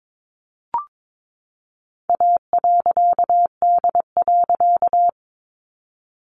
Morse code, heard as a keyed tone, about 22 wpm, 700 Hz. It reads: A.D.
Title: A Sound Of Thunder 22wpm